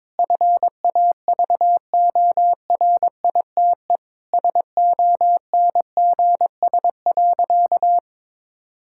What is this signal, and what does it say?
Morse code, heard as a keyed tone, about 22 wpm, 700 Hz. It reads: FAVORITE SONGS.
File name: practice / 22wpm / Sentences with 4-Word Phrases - 22wpm